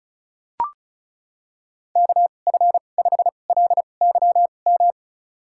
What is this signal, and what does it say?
Morse code, heard as a keyed tone, about 36 wpm, spaced as for 25 wpm, 700 Hz. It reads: KF5LYM
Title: USA Call Signs - 2x Character Spacing - 35wpm